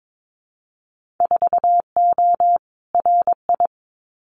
Morse code, heard as a keyed tone, about 22 wpm, 700 Hz. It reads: 4O RI
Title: Winter Field Day Exchanges Encore 22wpm